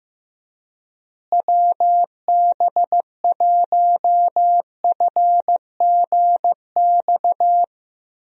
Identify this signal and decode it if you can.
Morse code, heard as a keyed tone, about 16 wpm, 700 Hz. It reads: WB1FGX
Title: USA Call Signs - 6 Characters - Repeated 3x - 15wpm